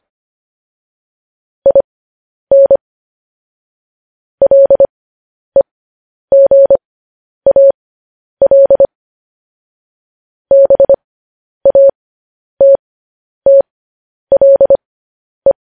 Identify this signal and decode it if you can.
Morse code, heard as a keyed tone, about 25 wpm, spaced as for 10 wpm, 550 Hz. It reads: IN LEGAL BATTLE
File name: podcast / News headlines-1x-25x10